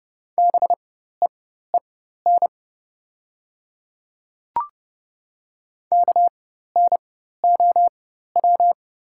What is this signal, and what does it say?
Morse code, heard as a keyed tone, about 30 wpm, spaced as for 14 wpm, 700 Hz. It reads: BEEN KNOW
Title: 4x Character Spacing - Top 100 Words - 30wpm